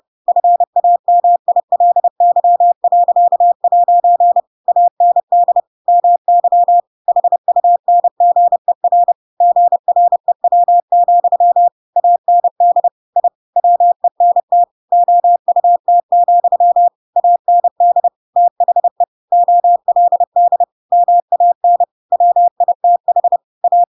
Morse code, heard as a keyed tone, about 30 wpm, 700 Hz. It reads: FAMILY.' AND MY HUNGER GREW, AND I WENT OUT, AND THE OLD MAN WITH A
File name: Pinocchio / Pinocchio ch7 30 30 ik1hge